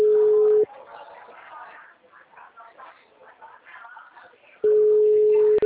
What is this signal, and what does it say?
Ringback tone (Europe, 425 Hz), with speech.